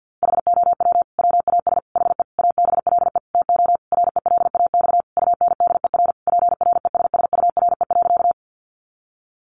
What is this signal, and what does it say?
Morse code, heard as a keyed tone, about 50 wpm, 700 Hz. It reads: HOW WAS HE ABLE TO RELAX UNDER PRESSURE.
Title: Sentences from Top 1400 Words 50wpm